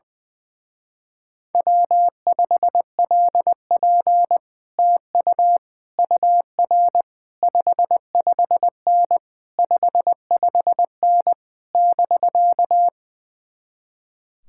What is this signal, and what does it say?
Morse code, heard as a keyed tone, about 20 wpm, 700 Hz. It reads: W5LP TU UR 55N 55N <BK>
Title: Speed-Racing Common QSO Phrases - 20wpm